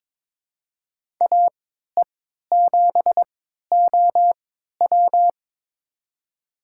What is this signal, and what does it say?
Morse code, heard as a keyed tone, about 22 wpm, spaced as for 13 wpm, 700 Hz. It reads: AE7OW
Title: USA Call Signs - 3x Character Spacing - 22wpm